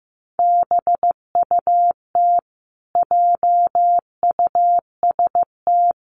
Morse code, heard as a keyed tone, about 15 wpm, 700 Hz. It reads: BUT JUST
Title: Sentences from Top 1100 Words 15wpm